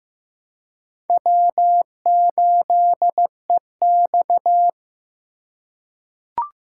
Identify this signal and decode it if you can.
Morse code, heard as a keyed tone, about 16 wpm, 700 Hz. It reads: W8EX
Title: USA Call Signs - 4 Characters - 2x Character Spacing - 15wpm